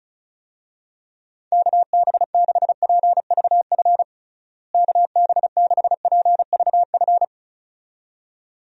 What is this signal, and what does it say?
Morse code, heard as a keyed tone, about 36 wpm, 700 Hz. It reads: KB6PVF KB6PVF
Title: USA Call Signs - 6 Characters - Repeated 2x - 35wpm